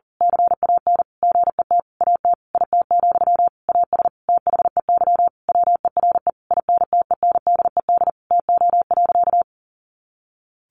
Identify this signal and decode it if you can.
Morse code, heard as a keyed tone, about 40 wpm, 700 Hz. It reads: CAN GET AT IT, AS THEY WERE INTENDED TO.
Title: A Sound Of Thunder 40wpm